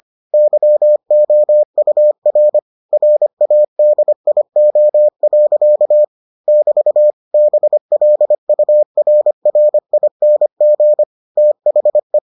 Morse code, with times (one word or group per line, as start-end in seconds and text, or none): 0.3-2.6 YOUR
2.9-6.0 RADIO.
6.5-7.1 <BT>
7.3-11.0 BLURRING
11.4-12.2 THE